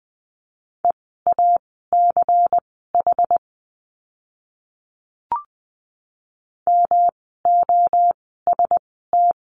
Morse code, heard as a keyed tone, about 20 wpm, spaced as for 14 wpm, 700 Hz. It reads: EACH MOST
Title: 192 Lesson 31f - Letter v - Words 20wpm